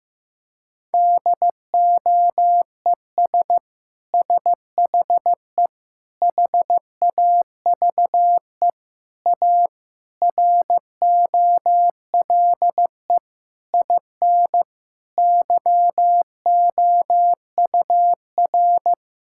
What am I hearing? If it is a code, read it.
Morse code, heard as a keyed tone, about 15 wpm, 700 Hz. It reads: DOES SHE HAVE A ROLE IN YOUR
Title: Sentences from Top 1000 Words - Review Part 6 - 15wpm